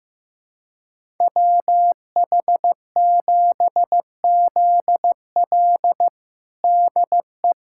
Morse code, heard as a keyed tone, about 15 wpm, 700 Hz. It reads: WH7ZL DE
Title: USA Callsign DE Callsign - 5 Characters - 15wpm